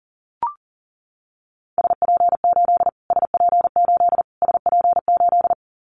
Morse code, heard as a keyed tone, about 40 wpm, 700 Hz. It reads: SP8 SP8 SP8
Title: Callsign Prefixes for Top 25 DXCC - Repeated 3x - 40wpm